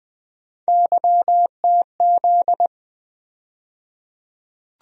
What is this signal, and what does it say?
Morse code, heard as a keyed tone, about 20 wpm, 700 Hz. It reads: YTZ
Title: USA Call Sign Postfixes - 3 Letters 20wpm